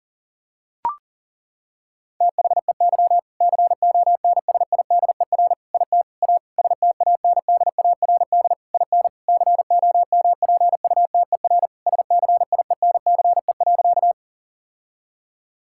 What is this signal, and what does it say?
Morse code, heard as a keyed tone, about 37 wpm, 700 Hz. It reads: THEY CONSIDER IT A STANDARD IN COMPUTER SCIENCE.
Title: Sentences from Top 1000 Words - Review Part 6 - 40wpm